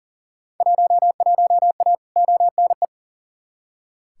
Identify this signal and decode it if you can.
Morse code, heard as a keyed tone, about 38 wpm, 700 Hz. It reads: 11A ONE
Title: Field Day Exchanges Encore - 40wpm